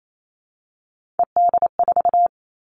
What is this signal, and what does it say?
Morse code, heard as a keyed tone, about 28 wpm, 700 Hz. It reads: ED4
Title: All Cut Numbers - 3-Digit 28wpm